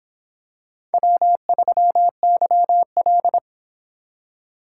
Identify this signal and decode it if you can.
Morse code, heard as a keyed tone, about 22 wpm, 700 Hz. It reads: W3YL